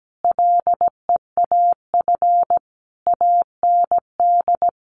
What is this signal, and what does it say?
Morse code, heard as a keyed tone, about 17 wpm, 700 Hz. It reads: LEAF AND
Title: A Sound Of Thunder 17wpm